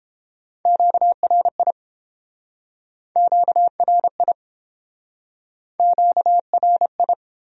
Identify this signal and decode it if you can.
Morse code, heard as a keyed tone, about 29 wpm, 700 Hz. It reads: QRI QRI QRI